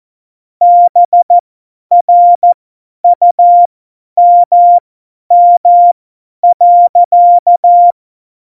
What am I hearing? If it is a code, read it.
Morse code, heard as a keyed tone, about 14 wpm, spaced as for 10 wpm, 700 Hz. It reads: BRUMM.
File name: Morsekurs / Witze / Witz 23. Januar 2018, 50 F70